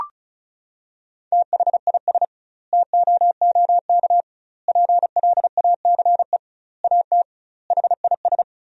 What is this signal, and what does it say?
Morse code, heard as a keyed tone, about 34 wpm, 700 Hz. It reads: THIS TOOK PLACE AT HIS
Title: 2x Word-Spacing - Sentences from Top 300 Words - 35wpm